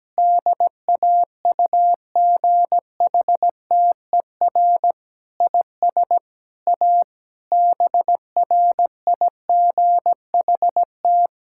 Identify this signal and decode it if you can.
Morse code, heard as a keyed tone, about 17 wpm, 700 Hz. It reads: DAUGHTER IS A BRIGHT
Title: Sentences from Top 1600 Words 17wpm